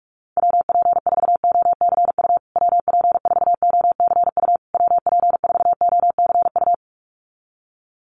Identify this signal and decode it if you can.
Morse code, heard as a keyed tone, about 44 wpm, 690 Hz. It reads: WP4OCU WP4OCU WP4OCU
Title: USA Call Signs Repeated 3x 45wpm